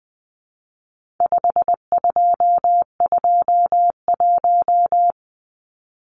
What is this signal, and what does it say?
Morse code, heard as a keyed tone, about 20 wpm, 700 Hz. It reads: 5221
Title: Numbers - 4-Digit 20wpm